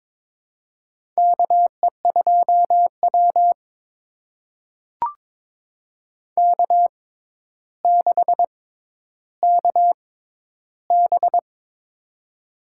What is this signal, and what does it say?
Morse code, heard as a keyed tone, about 22 wpm, spaced as for 11 wpm, 700 Hz. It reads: KE2W K6KB
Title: USA Call Signs - 4 Characters - 6x Character Spacing - 22wpm